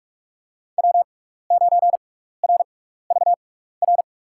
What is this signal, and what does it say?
Morse code, heard as a keyed tone, about 42 wpm, spaced as for 15 wpm, 700 Hz. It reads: W9RVR
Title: USA Call Signs - 6x Character Spacing - 45wpm